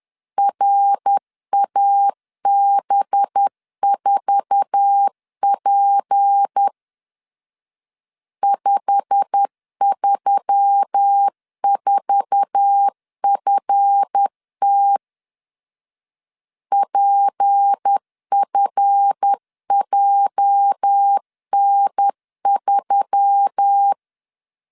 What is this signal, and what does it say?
Morse code, heard as a keyed tone, about 10 wpm, 800 Hz. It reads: RAB4P 534FT PFJN3
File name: LEZIONI / 9 2 10wpm 2736